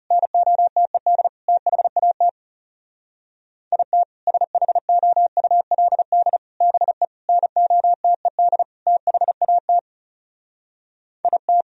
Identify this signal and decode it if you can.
Morse code, heard as a keyed tone, about 36 wpm, 700 Hz. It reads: NOTED THAT IT SHOULD BE NOTED THAT IT
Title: Speed-Racing - Sets of 5 Words 30wpm